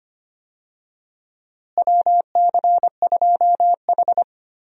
Morse code, heard as a keyed tone, about 25 wpm, 700 Hz. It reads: WC2H
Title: USA Call Signs - 4 Characters - Repeated 2x - 25wpm